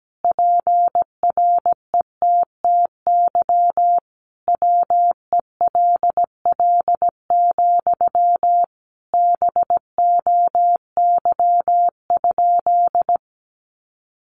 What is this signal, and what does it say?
Morse code, heard as a keyed tone, about 17 wpm, 700 Hz. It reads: PRETTY WELL, BOY?
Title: Sentences from Top 1000 Words 17wpm